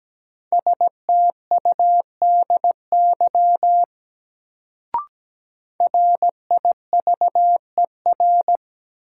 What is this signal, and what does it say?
Morse code, heard as a keyed tone, about 17 wpm, 700 Hz. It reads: STUDY RIVER
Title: Sentences from Top 300 Words - New Word Drill - 17wpm